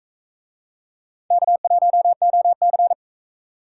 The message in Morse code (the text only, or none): K1OC